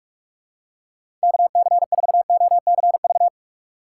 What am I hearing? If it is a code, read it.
Morse code, heard as a keyed tone, about 46 wpm, 700 Hz. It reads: KC4OCV
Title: USA Call Signs - 6 Characters - 4x Character Spacing - 45wpm